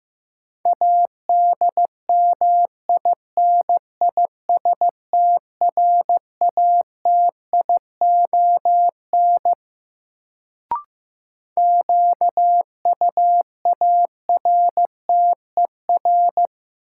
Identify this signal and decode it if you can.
Morse code, heard as a keyed tone, about 16 wpm, 700 Hz. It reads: ADMINISTRATION QUARTER
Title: Sentences from Top 1300 Words - New Word Drill Alt - 15wpm